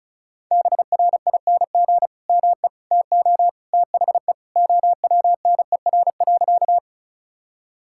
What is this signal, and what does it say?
Morse code, heard as a keyed tone, about 36 wpm, 700 Hz. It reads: BRING ME TO THE OWNER.
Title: Sentences from Top 1600 Words 35wpm